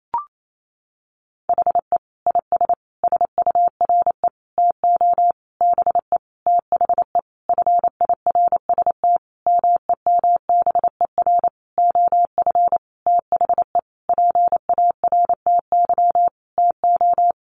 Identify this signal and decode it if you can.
Morse code, heard as a keyed tone, about 28 wpm, 700 Hz. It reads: HE IS SURE TO BE THE FIRST MEMBER OF THE PARTY TO
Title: Sentences from Top 1400 Words 28wpm